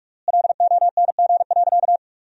Morse code, heard as a keyed tone, about 44 wpm, 700 Hz. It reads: LONG.
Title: 3x Word-Spacing - Sentences from Top 100 Words - 45wpm